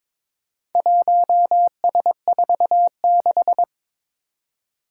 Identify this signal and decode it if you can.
Morse code, heard as a keyed tone, about 22 wpm, 700 Hz. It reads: 1S46
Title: Four Challenging Letter-Number Combinations 22wpm